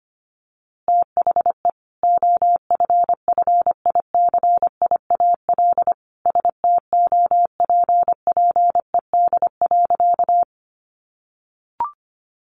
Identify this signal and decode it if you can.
Morse code, heard as a keyed tone, about 25 wpm, 700 Hz. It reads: THE OFFICIAL STOPPED.
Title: A Sound Of Thunder 25wpm